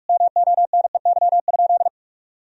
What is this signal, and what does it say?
Morse code, heard as a keyed tone, about 46 wpm, 700 Hz. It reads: MONEY?
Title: Sentences from Top 1000 Words - Review Part 4 - 45wpm